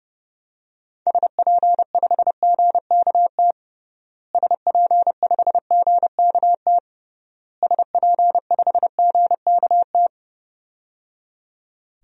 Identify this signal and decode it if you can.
Morse code, heard as a keyed tone, about 31 wpm, 700 Hz. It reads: SP5GKT SP5GKT SP5GKT
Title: DXCC-100 Call Signs - 6 Characters - Repeated 3x - 30wpm